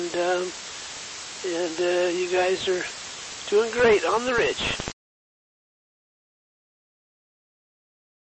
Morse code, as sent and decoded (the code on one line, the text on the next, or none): none